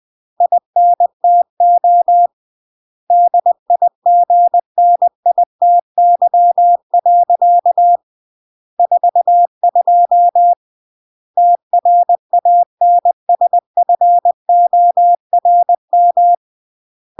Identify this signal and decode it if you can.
Morse code, heard as a keyed tone, about 20 wpm, 700 Hz. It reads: INTO DIGNITY. 42 TRANSFORM
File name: CW / jodorowsky / 82 commandments 20-20